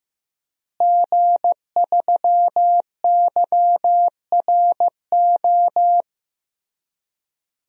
Morse code, heard as a keyed tone, about 15 wpm, 700 Hz. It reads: G3YRO
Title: DXCC-100 Call Signs - 5 Characters - Repeated 3x - 15wpm